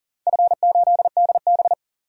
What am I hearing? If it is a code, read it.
Morse code, heard as a keyed tone, about 40 wpm, 700 Hz. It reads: F8DB
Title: DXCC-100 Call Signs - 4 Characters - 40wpm